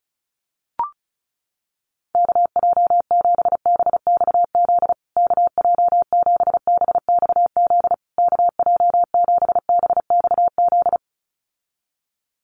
Morse code, heard as a keyed tone, about 34 wpm, 700 Hz. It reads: KJ7BXZ KJ7BXZ KJ7BXZ
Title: USA Call Signs Repeated 3x 35wpm